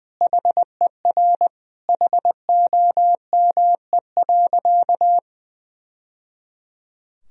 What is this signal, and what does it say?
Morse code, heard as a keyed tone, about 20 wpm, 700 Hz. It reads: HER HOME.
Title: Sentences from Top 1000 Words - Review Part 3 - 20wpm